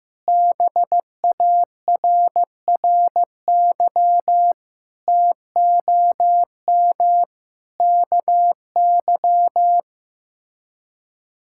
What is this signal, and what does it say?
Morse code, heard as a keyed tone, about 16 wpm, 700 Hz. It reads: BARRY TOM KY
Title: K1USN SST QSO Practice - 15wpm